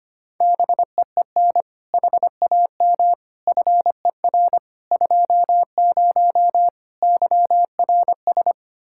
Morse code, heard as a keyed tone, about 25 wpm, 700 Hz. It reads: BEEN HAM FER 20 YRS
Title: Practice QSO - Exchange 3 - More Info - 25wpm